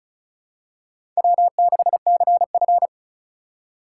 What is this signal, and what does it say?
Morse code, heard as a keyed tone, about 36 wpm, 700 Hz. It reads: W6CF
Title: USA Call Signs - 4 Characters - 2x Character Spacing - 35wpm